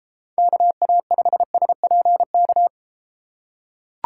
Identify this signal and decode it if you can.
Morse code, heard as a keyed tone, about 32 wpm, 700 Hz. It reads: KA5SPK